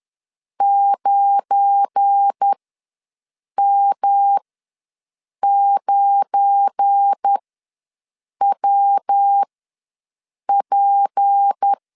Morse code, heard as a keyed tone, about 10 wpm, spaced as for 6 wpm, 800 Hz. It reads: 9M9WP
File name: LEZIONI / 7 GF9Z / 7 GF9 ws 10wpm 4034